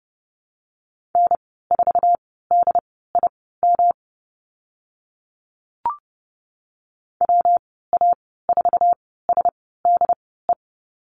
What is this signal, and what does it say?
Morse code, heard as a keyed tone, about 30 wpm, spaced as for 17 wpm, 700 Hz. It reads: N4DIM WA4SDE
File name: Lesson 19g - Letter w - Call Signs 30wpm